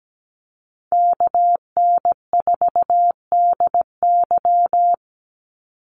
Morse code, heard as a keyed tone, about 17 wpm, 700 Hz. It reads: KN4DY
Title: USA Call Signs 17wpm